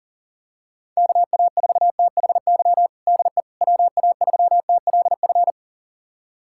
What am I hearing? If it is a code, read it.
Morse code, heard as a keyed tone, about 40 wpm, 700 Hz. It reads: KA4THY DE WA3TLF
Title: USA Callsign DE Callsign 40wpm